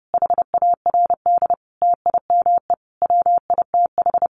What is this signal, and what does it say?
Morse code, heard as a keyed tone, about 30 wpm, 700 Hz. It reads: HARD TIME WITH
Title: Sentences from Top 300 Words 30wpm